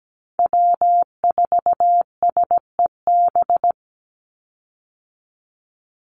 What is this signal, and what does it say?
Morse code, heard as a keyed tone, about 17 wpm, 700 Hz. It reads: W4SEB
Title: USA Call Signs 17wpm